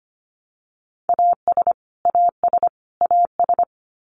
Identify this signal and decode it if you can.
Morse code, heard as a keyed tone, about 25 wpm, 700 Hz. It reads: AS AS AS
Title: Top 500 Words Repeated 25wpm